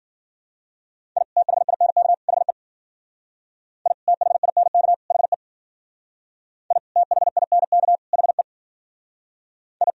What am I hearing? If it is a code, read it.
Morse code, heard as a keyed tone, about 52 wpm, 700 Hz. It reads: I THINK HE I THINK HE I THINK HE I